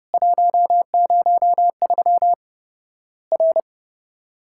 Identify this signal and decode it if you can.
Morse code, heard as a keyed tone, about 30 wpm, 700 Hz. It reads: 103 R
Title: Worked All Europe DX Contest - QTC Exchange - 30wpm